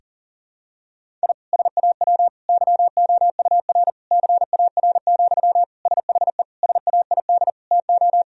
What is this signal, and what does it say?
Morse code, heard as a keyed tone, about 40 wpm, 700 Hz. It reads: I SAW YOUR CAR, SHE SAID TO
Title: Sentences from Top 1000 Words - Review Part 1 - 40wpm